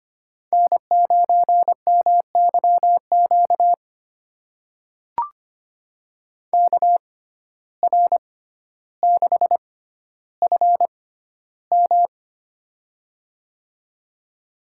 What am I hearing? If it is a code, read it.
Morse code, heard as a keyed tone, about 25 wpm, spaced as for 13 wpm, 700 Hz. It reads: N9MYQ KR6FM
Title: USA Call Signs - 5 Characters - 6x Character Spacing - 25wpm